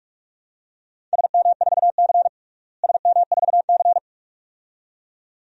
Morse code, heard as a keyed tone, about 46 wpm, 700 Hz. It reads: SM4C SM4C
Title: DXCC-100 Call Signs - 4 Characters - Repeated 2x - 45wpm